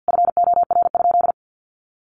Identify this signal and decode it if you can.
Morse code, heard as a keyed tone, about 50 wpm, 700 Hz. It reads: FOR?
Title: Sentences from Top 100 Words 50wpm